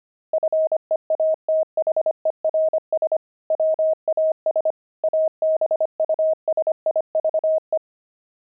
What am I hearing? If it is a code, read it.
Morse code, heard as a keyed tone, about 25 wpm, 630 Hz. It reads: FEATHERS WAS ABUSIVE
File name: pangram-fs-25-cs-25-journey